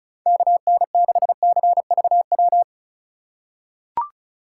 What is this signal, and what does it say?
Morse code, heard as a keyed tone, about 36 wpm, 690 Hz. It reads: KN6CVW
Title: USA Call Signs - 6 Characters - 3x Character Spacing - 35wpm